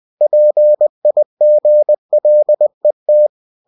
Morse code, heard as a keyed tone, about 20 wpm, 600 Hz. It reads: PIGLET
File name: cw-books / winnie-the-pooh / winnie-20wpm-page-0034